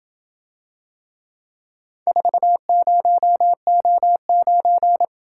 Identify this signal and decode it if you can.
Morse code, heard as a keyed tone, about 27 wpm, 700 Hz. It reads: 40O9